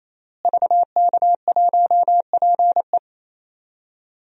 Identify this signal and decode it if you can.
Morse code, heard as a keyed tone, about 27 wpm, 700 Hz. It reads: VK1PE
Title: DXCC-100 Call Signs - 5 Characters - Repeated 3x - 28wpm